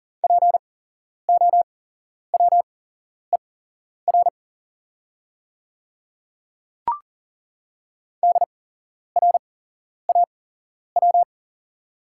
Morse code, heard as a keyed tone, about 39 wpm, spaced as for 11 wpm, 700 Hz. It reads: POWER DRAW
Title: 8x Character Spacing - Top 500 Words - 40wpm